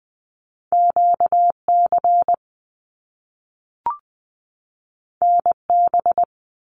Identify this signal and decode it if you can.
Morse code, heard as a keyed tone, about 20 wpm, 700 Hz. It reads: QC NB
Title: Canadian Provinces and Territories 20wpm